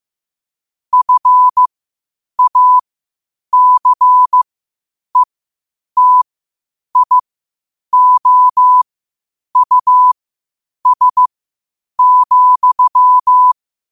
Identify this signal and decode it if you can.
Morse code, heard as a keyed tone, about 15 wpm, spaced as for 8 wpm, 1000 Hz. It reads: FACETIOUS,
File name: qotdfiles / 8wpm